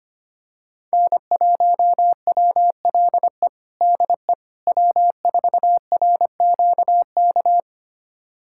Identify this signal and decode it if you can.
Morse code, heard as a keyed tone, about 25 wpm, 700 Hz. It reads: N1WLE DE W4RQK
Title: USA Callsign DE Callsign - 5 Characters - 25wpm